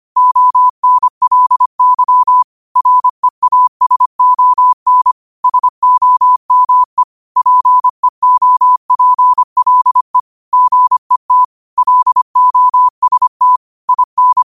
Morse code, heard as a keyed tone, about 25 wpm, 1000 Hz. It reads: ONLY REASON SOME PEOPLE GET LOST IN
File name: qotdfiles / 25wpm